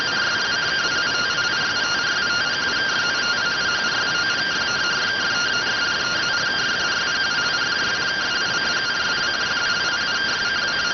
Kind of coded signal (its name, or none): two-tone teleprinter signal